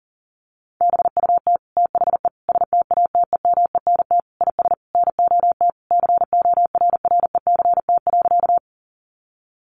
Morse code, heard as a keyed tone, about 40 wpm, 700 Hz. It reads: BUT THE STATEMENT IS NOT CORRECT.
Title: Sentences from Top 1200 Words 40wpm